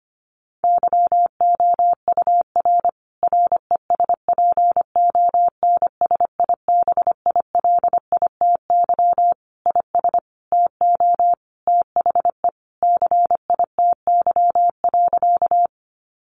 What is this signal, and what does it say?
Morse code, heard as a keyed tone, about 25 wpm, 700 Hz. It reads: YOUR RESPONSIBILITY IS TO THE CITY.
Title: Sentences from Top 1200 Words 25wpm